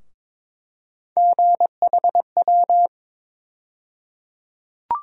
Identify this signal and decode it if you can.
Morse code, heard as a keyed tone, about 22 wpm, 700 Hz. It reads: GHW
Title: USA Call Sign Postfixes - 3 Letters 22wpm